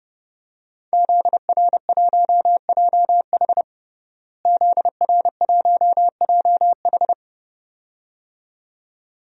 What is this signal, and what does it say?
Morse code, heard as a keyed tone, about 30 wpm, 700 Hz. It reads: ZR1JH ZR1JH
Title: DXCC-100 Call Signs - 5 Characters - Repeated 2x - 30wpm